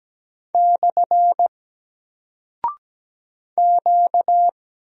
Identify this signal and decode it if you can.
Morse code, heard as a keyed tone, about 17 wpm, 700 Hz. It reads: / Q
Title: Single Character (Letter:Number:Punctuation:Prosign) 17wpm